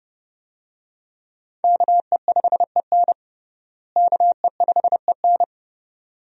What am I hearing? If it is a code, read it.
Morse code, heard as a keyed tone, about 32 wpm, 700 Hz. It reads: KE5EN KE5EN
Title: USA Call Signs - 5 Characters - Repeated 2x - 30wpm